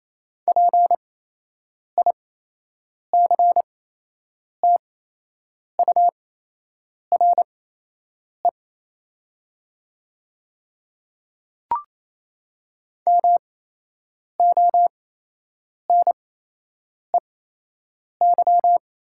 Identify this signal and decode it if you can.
Morse code, heard as a keyed tone, about 28 wpm, spaced as for 8 wpm, 700 Hz. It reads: PICTURE MONEY
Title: 8x Character Spacing - Top 500 Words - 28wpm